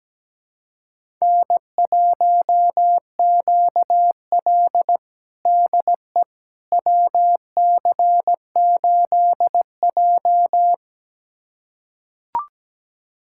Morse code, heard as a keyed tone, about 17 wpm, 700 Hz. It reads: N1QL DE WC8J
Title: USA Callsign DE Callsign - 4 Characters - 17wpm